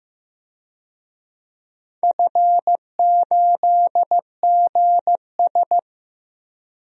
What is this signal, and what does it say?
Morse code, heard as a keyed tone, about 15 wpm, 700 Hz. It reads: F8GS
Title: DXCC-100 Call Signs - 4 Characters - 15wpm